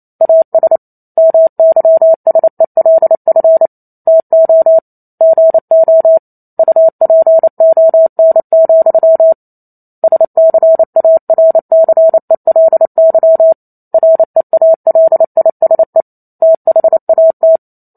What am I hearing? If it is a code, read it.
Morse code, heard as a keyed tone, about 28 wpm, 650 Hz. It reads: AS MYSELF TO GO UPON, SCARCELY REALISE THAT